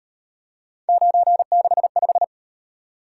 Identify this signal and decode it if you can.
Morse code, heard as a keyed tone, about 38 wpm, 700 Hz. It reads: 965